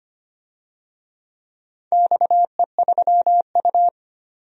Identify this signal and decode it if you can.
Morse code, heard as a keyed tone, about 25 wpm, 700 Hz. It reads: XE3U
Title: DXCC-100 Call Signs - 4 Characters - 25wpm